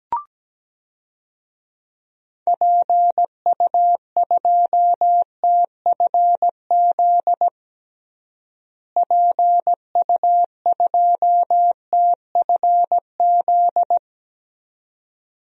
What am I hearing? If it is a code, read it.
Morse code, heard as a keyed tone, about 17 wpm, 700 Hz. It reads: PU2TFZ PU2TFZ
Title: DXCC-100 Call Signs - 6 Characters - Repeated 2x - 17wpm